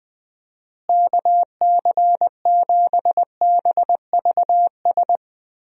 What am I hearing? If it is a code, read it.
Morse code, heard as a keyed tone, about 21 wpm, 700 Hz. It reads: KC7BVS
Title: USA Call Signs - 6 Characters - Repeated 2x - 20wpm